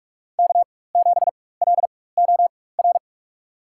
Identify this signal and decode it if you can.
Morse code, heard as a keyed tone, about 46 wpm, spaced as for 21 wpm, 700 Hz. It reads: K7LOR
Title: USA Call Signs - 5 Characters - 4x Character Spacing - 45wpm